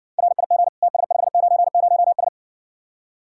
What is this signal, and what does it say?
Morse code, heard as a keyed tone, about 60 wpm, 700 Hz. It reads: RIG TS590S